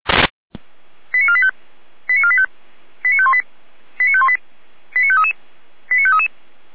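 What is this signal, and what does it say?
Five-tone selective call (ZVEI), heard as quick runs of five tones, tones of about 70 ms. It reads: 89485 89485 89518 89518 89533 89533
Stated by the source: ZVEI five-tone selective call received by SDR.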